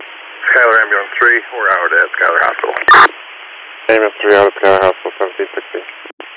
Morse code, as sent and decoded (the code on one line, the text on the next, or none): none